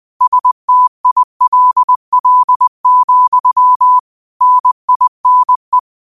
Morse code, heard as a keyed tone, about 20 wpm, 1000 Hz. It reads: STILL, NINE
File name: qotdfiles / 20wpm